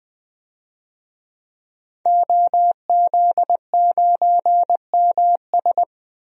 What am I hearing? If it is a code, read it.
Morse code, heard as a keyed tone, about 20 wpm, 700 Hz. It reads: OZ9MS
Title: DXCC-100 Call Signs - 5 Characters - 20wpm